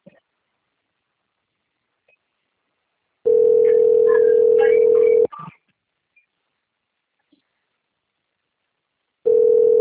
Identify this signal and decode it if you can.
Ringback tone (North America), 440 + 480 Hz.
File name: out- - - - - .25931